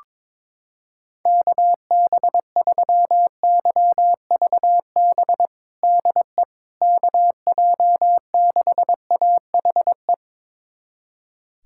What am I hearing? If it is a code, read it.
Morse code, heard as a keyed tone, about 22 wpm, 700 Hz. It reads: KB3YVB DE KJ6AHE
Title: USA Callsign DE Callsign - 6 Characters - 22wpm